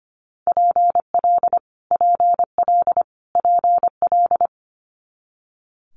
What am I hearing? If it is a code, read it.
Morse code, heard as a keyed tone, about 25 wpm, 700 Hz. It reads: PL PL PL
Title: Multi-letter Phonemes Repeated 3x 25wpm